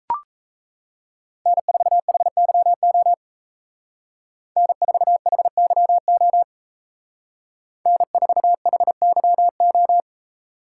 Morse code, heard as a keyed tone, about 35 wpm, 700 Hz. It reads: N4HYO N4HYO N4HYO